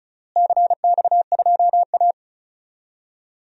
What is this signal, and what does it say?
Morse code, heard as a keyed tone, about 36 wpm, 700 Hz. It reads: CX2A
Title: DXCC-100 Call Signs - 4 Characters - Repeated 3x - 35wpm